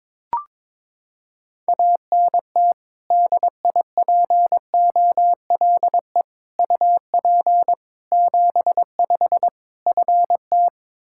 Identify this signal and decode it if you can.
Morse code, heard as a keyed tone, about 22 wpm, 700 Hz. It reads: ANT DIPOLE UP 75 FT
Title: Practice QSO - Exchange 2 - Info - 22wpm